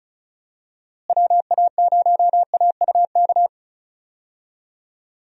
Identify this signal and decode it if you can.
Morse code, heard as a keyed tone, about 34 wpm, 700 Hz. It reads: WA0AUK
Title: USA Call Signs - 6 Characters - Repeated 2x - 35wpm